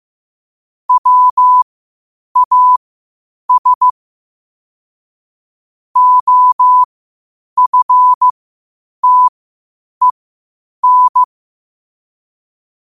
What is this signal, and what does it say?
Morse code, heard as a keyed tone, about 15 wpm, spaced as for 8 wpm, 1000 Hz. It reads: WAS OFTEN
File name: qotdfiles / 8wpm